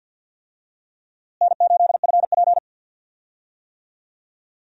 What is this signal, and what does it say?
Morse code, heard as a keyed tone, about 51 wpm, 700 Hz. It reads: N8FP
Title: USA Call Signs - 4 Characters - Repeated 3x - 50wpm